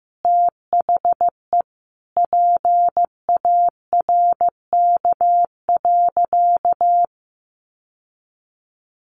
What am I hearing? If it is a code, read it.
Morse code, heard as a keyed tone, about 15 wpm, 700 Hz. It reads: THE PARK.
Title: Sentences from Top 1300 Words 15wpm